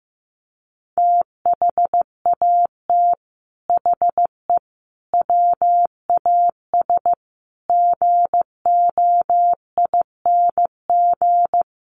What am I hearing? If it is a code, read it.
Morse code, heard as a keyed tone, about 15 wpm, 700 Hz. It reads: THAT HE WAS GOING
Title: Sets of 5 Words 15wpm